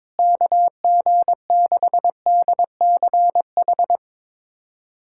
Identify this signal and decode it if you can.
Morse code, heard as a keyed tone, about 21 wpm, 700 Hz. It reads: KG6DCH
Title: USA Callsign DE Callsign 22wpm